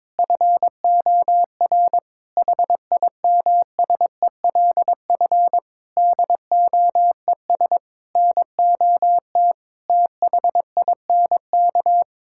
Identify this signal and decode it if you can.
Morse code, heard as a keyed tone, about 22 wpm, 700 Hz. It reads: FOR HIMSELF DOES NOT THINK
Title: Sentences from Top 1600 Words 22wpm